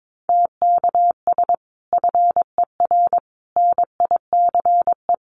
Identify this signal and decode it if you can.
Morse code, heard as a keyed tone, about 22 wpm, 700 Hz. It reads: TKS FER NICE
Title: Common QSO Phrases 22wpm